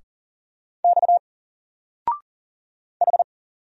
Morse code, heard as a keyed tone, about 40 wpm, 700 Hz. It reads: X H
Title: Single Letter-Number 40wpm